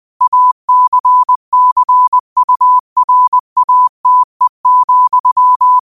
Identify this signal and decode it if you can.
Morse code, heard as a keyed tone, about 20 wpm, 1000 Hz. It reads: ACCURATE,
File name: qotdfiles / 20wpm